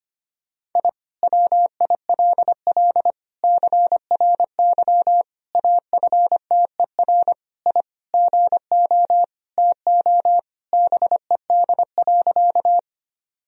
Morse code, heard as a keyed tone, about 25 wpm, 700 Hz. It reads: I WILL CRY AFTER I GO TO BED.
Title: Sentences from Top 1000 Words - Review Part 2 - 25wpm